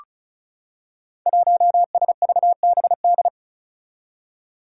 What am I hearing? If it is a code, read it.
Morse code, heard as a keyed tone, about 36 wpm, 700 Hz. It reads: 1SVBD
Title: Five Challenging Letter-Number Combinations 35wpm